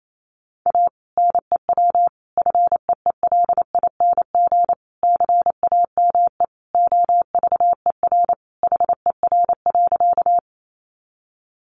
Morse code, heard as a keyed tone, about 28 wpm, 700 Hz. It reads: A NEW FEELING CAME OVER HER.
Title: Sentences from Top 1200 Words 28wpm